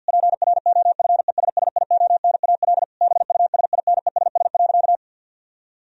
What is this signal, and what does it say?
Morse code, heard as a keyed tone, about 50 wpm, 700 Hz. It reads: PROFESSIONAL BUSINESS.
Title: Sentences from Top 1000 Words - Review Part 4 - 50wpm